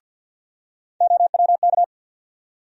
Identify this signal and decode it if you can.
Morse code, heard as a keyed tone, about 52 wpm, 700 Hz. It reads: OWK
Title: USA Call Sign Postfixes - 3 Letters 50wpm